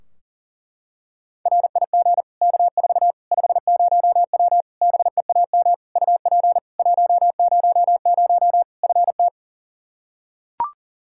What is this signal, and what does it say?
Morse code, heard as a keyed tone, about 40 wpm, 700 Hz. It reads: RIG K4 50W BEAM UP 100 FT
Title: Practice QSO - Exchange 2 - Info - 40wpm